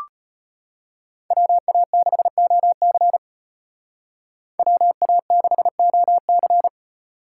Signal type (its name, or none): Morse code